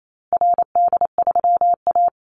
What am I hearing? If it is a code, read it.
Morse code, heard as a keyed tone, about 28 wpm, 700 Hz. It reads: RD3A
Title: Call Signs Repeated 2x - 28wpm